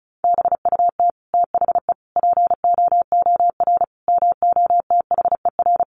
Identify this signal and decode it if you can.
Morse code, heard as a keyed tone, about 34 wpm, 700 Hz. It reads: BUT THE POOR MOTHER
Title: Sentences from Top 1000 Words 35wpm